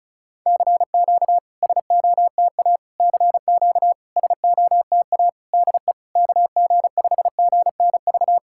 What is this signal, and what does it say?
Morse code, heard as a keyed tone, about 36 wpm, 700 Hz. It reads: CQ SOTA CQ SOTA DE KG5GNV
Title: SOTA QSO Practice 35wpm